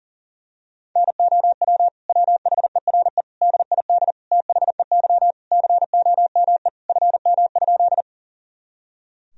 Morse code, heard as a keyed tone, about 40 wpm, 700 Hz. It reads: NOW WHERE DID THEY COME FM?
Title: Sentences from Top 200 Words and Abbreviations 40wpm